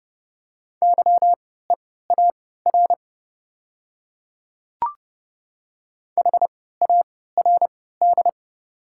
Morse code, heard as a keyed tone, about 30 wpm, spaced as for 17 wpm, 690 Hz. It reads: YEAR HARD
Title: 3x Character Spacing - Top 300 Words - 30wpm